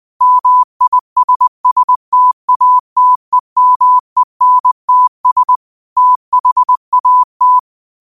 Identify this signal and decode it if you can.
Morse code, heard as a keyed tone, about 20 wpm, 1000 Hz. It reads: MISSTATEMENTS THAT